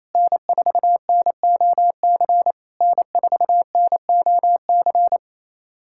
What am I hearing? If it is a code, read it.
Morse code, heard as a keyed tone, about 28 wpm, 700 Hz. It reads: N4NOC N4NOC
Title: SOTA QSO Practice 28wpm